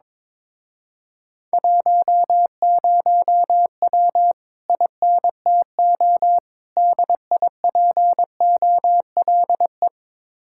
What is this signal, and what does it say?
Morse code, heard as a keyed tone, about 23 wpm, 700 Hz. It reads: 10W INTO DIPOLE
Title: Speed-Racing Common QSO Phrases Encore - 22wpm